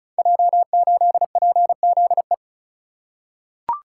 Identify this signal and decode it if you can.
Morse code, heard as a keyed tone, about 36 wpm, 690 Hz. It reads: J8PZE
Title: Code Groups - 5 Letter:Numbers - 35wpm